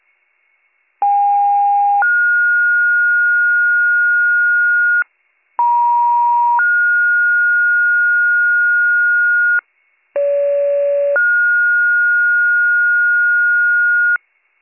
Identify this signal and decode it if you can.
Two-tone page, heard as a one-second tone, then a three-second tone, about 810 then 1430 Hz, about 950 then 1430 Hz, about 570 then 1430 Hz. It is heard from 1.0 to 14.2 s.